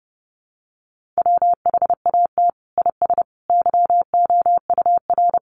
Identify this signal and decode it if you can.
Morse code, heard as a keyed tone, about 30 wpm, 700 Hz. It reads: WHAT IS YOUR
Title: Sentences from Top 600 Words 30wpm